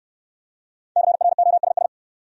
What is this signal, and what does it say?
Morse code, heard as a keyed tone, about 69 wpm, 700 Hz. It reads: CROSS